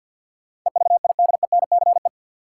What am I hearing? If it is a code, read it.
Morse code, heard as a keyed tone, about 53 wpm, 700 Hz. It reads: EVIDENCE
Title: Sentences from Top 1300 Words - New Word Drill Alt - 50wpm